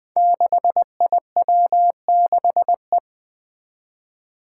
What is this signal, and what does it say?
Morse code, heard as a keyed tone, about 20 wpm, 700 Hz. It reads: 6IW6E
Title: Five Challenging Letter-Number Combinations 20wpm